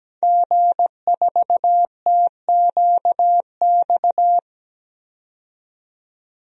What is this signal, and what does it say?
Morse code, heard as a keyed tone, about 17 wpm, 700 Hz. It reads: G4TQX
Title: DXCC-100 Call Signs - 5 Characters - 17wpm